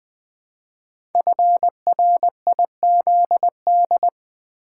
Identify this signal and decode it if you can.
Morse code, heard as a keyed tone, about 20 wpm, 700 Hz. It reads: FRIZD
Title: Code Groups - 5 Letter:Numbers - 20wpm